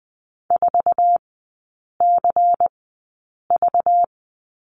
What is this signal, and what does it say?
Morse code, heard as a keyed tone, about 20 wpm, spaced as for 8 wpm, 690 Hz. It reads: 4CV
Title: 191 Lesson 31e - Letter v - Three Characters 20wpm